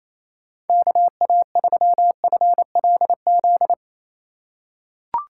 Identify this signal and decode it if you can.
Morse code, heard as a keyed tone, about 28 wpm, 700 Hz. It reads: KA3FLZ
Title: USA Call Signs - 6 Characters - 2x Character Spacing - 28wpm